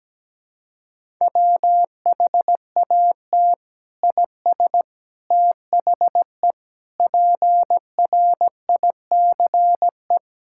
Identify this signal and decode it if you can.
Morse code, heard as a keyed tone, about 17 wpm, 700 Hz. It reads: WHAT IS THE PRICE
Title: Sentences from Top 1000 Words - Review Part 5 - 17wpm